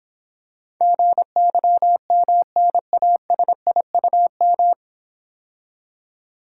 Morse code, heard as a keyed tone, about 26 wpm, 700 Hz. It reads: GYMNASIUM